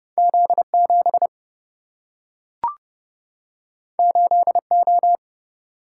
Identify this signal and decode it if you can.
Morse code, heard as a keyed tone, about 30 wpm, 700 Hz. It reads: Z 7 8 O
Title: Single-Letter Single-Number 30wpm